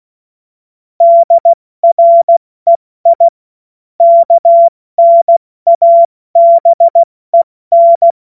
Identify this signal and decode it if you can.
Morse code, heard as a keyed tone, about 16 wpm, 680 Hz. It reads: DREI KNABEN
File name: Witz-14-Oktober-2025-70-F80